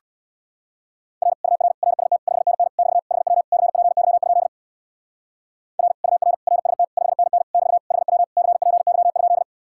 Morse code, heard as a keyed tone, about 58 wpm, 700 Hz. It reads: R UR RST 5NN <BT> HW CPY? R UR RST 5NN <BT> HW CPY?
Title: Speed-Racing Common QSO Phrases Encore - 50wpm